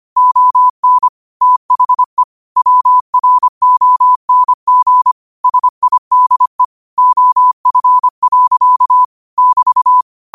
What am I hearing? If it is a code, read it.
Morse code, heard as a keyed tone, about 25 wpm, 1000 Hz. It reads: ON THE WRONG SIDE OF. <BT>
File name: qotdfiles / 25wpm